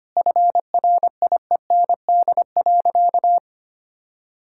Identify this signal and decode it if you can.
Morse code, heard as a keyed tone, about 25 wpm, 700 Hz. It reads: FRIEND.
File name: Easy Sentences with Hints - 25wpm